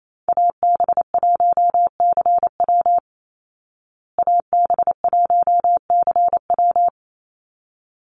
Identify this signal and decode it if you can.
Morse code, heard as a keyed tone, about 28 wpm, 700 Hz. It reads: AB1CW AB1CW
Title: Call Signs Repeated 3x - 28wpm